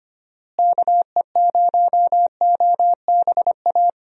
Morse code, heard as a keyed tone, about 25 wpm, 700 Hz. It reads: KE0OBA
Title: USA Call Signs - 6 Characters - Repeated 2x - 25wpm